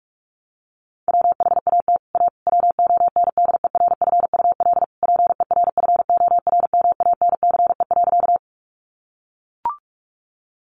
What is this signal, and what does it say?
Morse code, heard as a keyed tone, about 44 wpm, 700 Hz. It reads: WHAT A WONDERFUL PERFORMANCE.
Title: Sentences from Top 1100 Words 45wpm